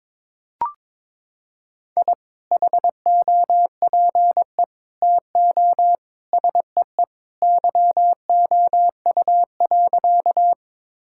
Morse code, heard as a keyed tone, about 22 wpm, 700 Hz. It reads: I HOPE TO SEE YOU.
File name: practice / 22wpm / Sentences with 2-Word Phrases - 22wpm